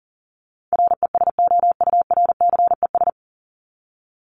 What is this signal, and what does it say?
Morse code, heard as a keyed tone, about 40 wpm, 700 Hz. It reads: RESOURCES
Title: ES:S Suffix - Top 100 Words with Base Words 40wpm